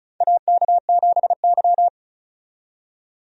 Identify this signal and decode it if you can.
Morse code, heard as a keyed tone, about 34 wpm, 700 Hz. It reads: AK7Y
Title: USA Call Signs - 4 Characters - 35wpm